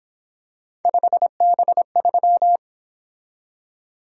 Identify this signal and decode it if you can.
Morse code, heard as a keyed tone, about 26 wpm, 700 Hz. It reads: 5B3
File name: Speed-Racing - Three Letter-Number Combinations - 22wpm